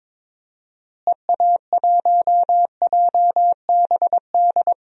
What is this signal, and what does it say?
Morse code, heard as a keyed tone, about 22 wpm, 700 Hz. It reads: EA1JBD
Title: DXCC-100 Call Signs - 6 Characters - Repeated 2x - 22wpm